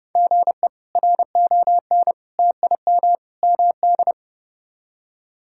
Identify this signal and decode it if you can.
Morse code, heard as a keyed tone, about 31 wpm, 700 Hz. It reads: GE RON TIM MD
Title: K1USN SST QSO Practice - 30wpm